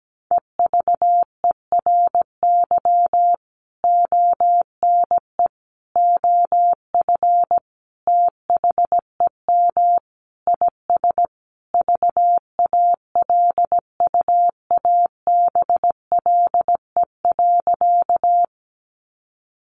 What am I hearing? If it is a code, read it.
Morse code, heard as a keyed tone, about 17 wpm, 700 Hz. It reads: EVERY ONE OF THEM IS VALUABLE.
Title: Sentences from Top 1100 Words 17wpm